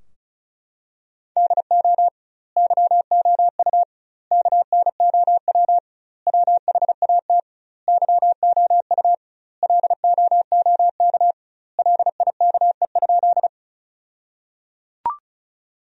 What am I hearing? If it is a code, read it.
Morse code, heard as a keyed tone, about 36 wpm, 700 Hz. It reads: DO YOU KNOW WHAT YOU LOOK LIKE?
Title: 2x Word-Spacing - Sentences from Top 100 Words - 35wpm